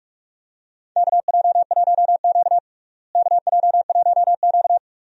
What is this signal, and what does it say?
Morse code, heard as a keyed tone, about 46 wpm, 700 Hz. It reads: KJ1Q KJ1Q
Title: USA Call Signs - 4 Characters - Repeated 2x - 45wpm